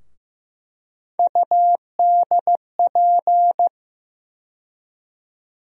Morse code, heard as a keyed tone, about 15 wpm, 700 Hz. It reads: UDP
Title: USA Call Sign Postfixes - 3 Letters 15wpm